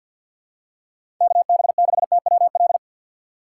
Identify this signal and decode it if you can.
Morse code, heard as a keyed tone, about 46 wpm, 700 Hz. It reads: KB6TWL
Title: USA Call Signs - 6 Characters - 6x Character Spacing - 50wpm